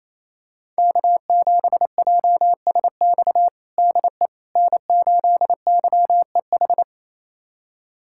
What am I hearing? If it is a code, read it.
Morse code, heard as a keyed tone, about 28 wpm, 700 Hz. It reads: K7JSX DE N8YEH
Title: USA Callsign DE Callsign - 5 Characters - 28wpm